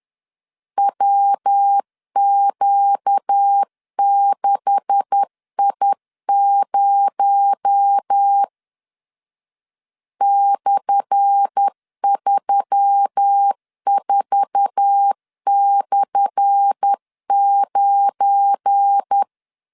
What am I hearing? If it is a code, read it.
Morse code, heard as a keyed tone, about 10 wpm, 800 Hz. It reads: WQ6I0 /34/9
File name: LEZIONI / 10 SIGNS / 10A SLASH 10wpm 2808